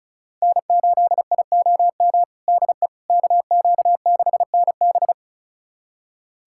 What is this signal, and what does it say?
Morse code, heard as a keyed tone, about 34 wpm, 700 Hz. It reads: N8IOM DE KQ6NB
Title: USA Callsign DE Callsign - 5 Characters - 35wpm